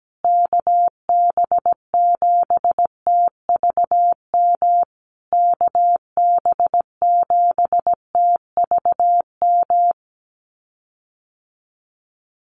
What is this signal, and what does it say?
Morse code, heard as a keyed tone, about 17 wpm, 700 Hz. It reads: KB7TVM KB7TVM
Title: USA Call Signs Repeated 3x 17wpm